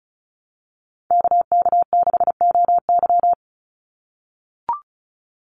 Morse code, heard as a keyed tone, about 34 wpm, 700 Hz. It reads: KK6OY
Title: USA Call Signs 35wpm